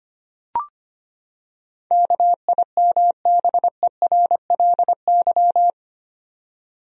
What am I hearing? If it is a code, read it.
Morse code, heard as a keyed tone, about 25 wpm, 700 Hz. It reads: KIMBERLY
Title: USA Common Names - 25wpm